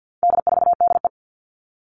Morse code, heard as a keyed tone, about 48 wpm, 700 Hz. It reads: D4DE
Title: All Cut Numbers - 4-Digit 50wpm